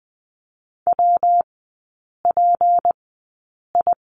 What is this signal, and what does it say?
Morse code, heard as a keyed tone, about 20 wpm, spaced as for 8 wpm, 700 Hz. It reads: WPI
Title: 152 Lesson 25e - Letter p - Three Characters 20wpm